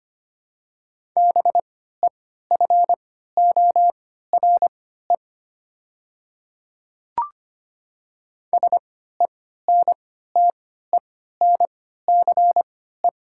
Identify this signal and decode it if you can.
Morse code, heard as a keyed tone, about 25 wpm, spaced as for 14 wpm, 700 Hz. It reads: BEFORE SENTENCE
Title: 3x Character Spacing - Top 200 Words - 25wpm